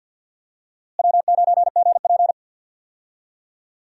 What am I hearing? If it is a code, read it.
Morse code, heard as a keyed tone, about 47 wpm, 700 Hz. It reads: W9GP
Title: USA Call Signs - 4 Characters - Repeated 2x - 50wpm